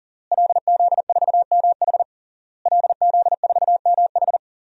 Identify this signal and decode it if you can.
Morse code, heard as a keyed tone, about 40 wpm, 700 Hz. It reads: LZ4MH LZ4MH
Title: DXCC-100 Call Signs - 5 Characters - Repeated 3x - 40wpm